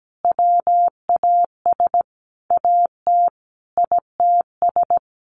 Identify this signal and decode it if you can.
Morse code, heard as a keyed tone, about 17 wpm, 700 Hz. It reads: WAS AT ITS
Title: Sentences from Top 1300 Words 17wpm